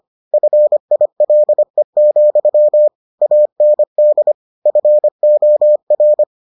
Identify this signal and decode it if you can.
Morse code, heard as a keyed tone, about 25 wpm, 600 Hz. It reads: FILE, AND FOR